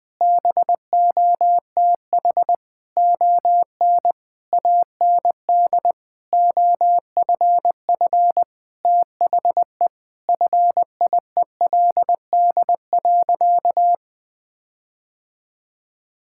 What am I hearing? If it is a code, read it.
Morse code, heard as a keyed tone, about 20 wpm, 700 Hz. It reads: BOTH ON AND OFF THE FIELD.
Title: Sentences with 6-Word Phrases - Spoken Beginning - 20wpm